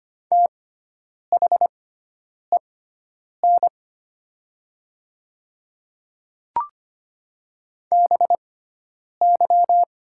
Morse code, heard as a keyed tone, about 25 wpm, spaced as for 9 wpm, 700 Hz. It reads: THEN BY
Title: 6x Character Spacing - Top 100 Words - 25wpm